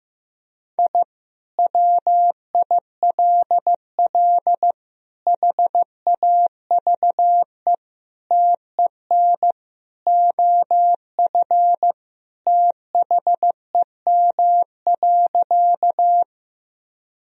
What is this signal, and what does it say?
Morse code, heard as a keyed tone, about 16 wpm, 700 Hz. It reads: I WILL HAVE TEN OF THEM.
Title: Sentences from Top 1000 Words - Review Part 2 - 15wpm